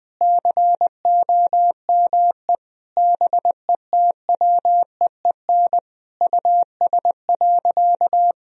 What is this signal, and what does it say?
Morse code, heard as a keyed tone, about 20 wpm, 700 Hz. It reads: COME BETWEEN US.
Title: Sentences from Top 1000 Words - Review Part 1 - 20wpm